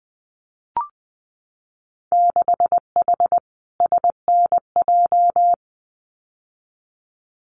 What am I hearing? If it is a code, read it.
Morse code, heard as a keyed tone, about 20 wpm, 700 Hz. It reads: 6H SNJ
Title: Winter Field Day Exchanges Encore 20wpm